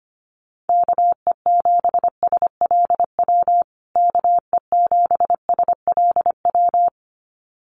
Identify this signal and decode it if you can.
Morse code, heard as a keyed tone, about 25 wpm, 700 Hz. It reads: KE7SLW KE7SLW
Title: USA Call Signs Repeated 2x 25wpm